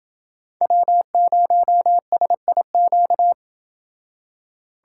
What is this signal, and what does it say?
Morse code, heard as a keyed tone, about 29 wpm, 700 Hz. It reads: W0SIQ